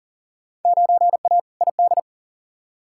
Morse code, heard as a keyed tone, about 40 wpm, 700 Hz. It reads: 9A ID
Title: Field Day Exchanges Encore - 40wpm